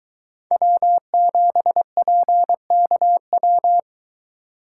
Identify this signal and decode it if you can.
Morse code, heard as a keyed tone, about 22 wpm, 700 Hz. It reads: W7PKW